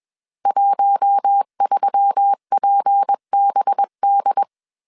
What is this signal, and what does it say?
Morse code, heard as a keyed tone, about 21 wpm, 800 Hz. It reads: 13PBD
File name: CW / DET / 1 6 1500C S C 20wpm 2116